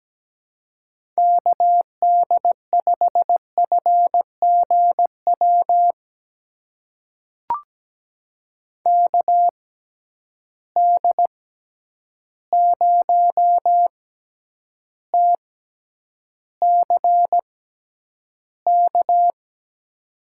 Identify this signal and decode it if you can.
Morse code, heard as a keyed tone, about 17 wpm, spaced as for 9 wpm, 700 Hz. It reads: KD5FGW KD0TCK
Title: USA Call Signs - 6 Characters - 6x Character Spacing - 17wpm